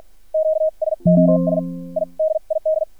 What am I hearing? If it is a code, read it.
Morse code, heard as a keyed tone, about 50 wpm, spaced as for 41 wpm, 630 Hz. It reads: QSOS I DID
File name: archived / CRSnet